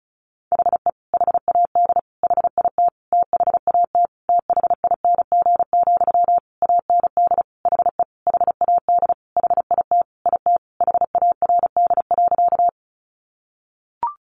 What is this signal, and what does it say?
Morse code, heard as a keyed tone, about 34 wpm, 700 Hz. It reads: HE HAD HIT THAT THING, AND HE HAD HIT IT HARD.
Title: Sentences from Top 1200 Words 35wpm